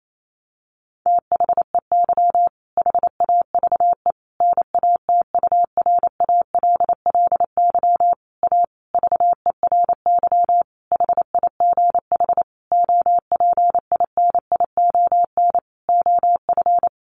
Morse code, heard as a keyed tone, about 28 wpm, 700 Hz. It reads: THEY HAVE NATURALLY A VERY HIGH OPINION OF
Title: Sentences from Top 1300 Words 28wpm